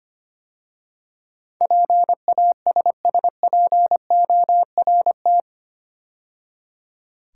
Morse code, heard as a keyed tone, about 25 wpm, 700 Hz. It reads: PASSPORT
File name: 8 Letter Words - First 3 Letters Given - 25wpm